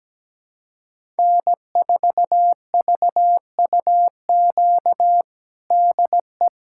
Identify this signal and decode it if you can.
Morse code, heard as a keyed tone, about 17 wpm, 700 Hz. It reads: N4VUQ DE
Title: USA Callsign DE Callsign 17wpm